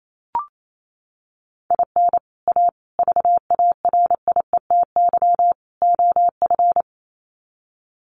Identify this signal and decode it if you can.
Morse code, heard as a keyed tone, about 28 wpm, 700 Hz. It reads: IN A VARIETY OF
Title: Sets of 4 Words 28wpm